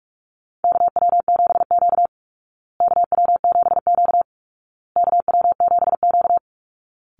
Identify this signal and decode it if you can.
Morse code, heard as a keyed tone, about 44 wpm, 700 Hz. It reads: KW7Q KW7Q KW7Q
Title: Call Signs Repeated 2x - 45wpm